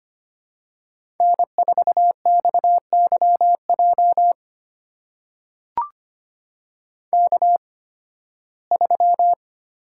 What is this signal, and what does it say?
Morse code, heard as a keyed tone, about 25 wpm, spaced as for 16 wpm, 700 Hz. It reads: N4XYJ K3
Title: USA Call Signs - 5 Characters - 8x Character Spacing - 25wpm